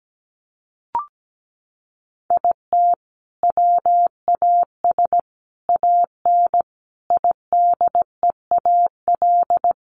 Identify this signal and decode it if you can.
Morse code, heard as a keyed tone, about 17 wpm, 700 Hz. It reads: IT WAS AN IDEAL
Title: Sentences from Top 1400 Words 17wpm